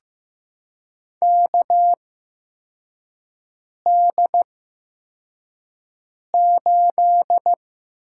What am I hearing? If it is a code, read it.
Morse code, heard as a keyed tone, about 15 wpm, spaced as for 4 wpm, 700 Hz. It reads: KD8
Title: USA Call Signs - 6 Characters - 8x Character Spacing - 15wpm